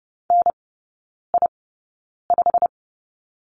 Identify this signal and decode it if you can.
Morse code, heard as a keyed tone, about 30 wpm, spaced as for 9 wpm, 700 Hz. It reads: NI5
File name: Lesson 18e - Letter m - Three Characters 30wpm